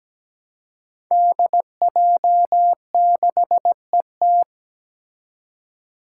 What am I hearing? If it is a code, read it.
Morse code, heard as a keyed tone, about 17 wpm, 700 Hz. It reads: DJ6ET
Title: Five Challenging Letter-Number Combinations 17wpm